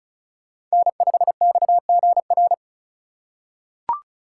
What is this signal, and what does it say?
Morse code, heard as a keyed tone, about 36 wpm, 690 Hz. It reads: N5XGR
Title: USA Call Signs - 5 Characters - 3x Character Spacing - 35wpm